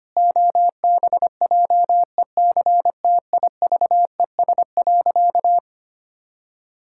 Morse code, heard as a keyed tone, about 25 wpm, 700 Hz. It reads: OBJECTIVES.
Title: Pangram Challenge v2 - 25wpm